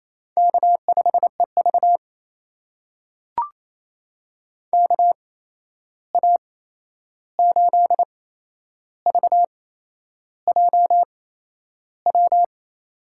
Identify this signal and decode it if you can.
Morse code, heard as a keyed tone, about 27 wpm, spaced as for 10 wpm, 700 Hz. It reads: K5EV KA8VJW
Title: USA Call Signs - 8x Character Spacing - 28wpm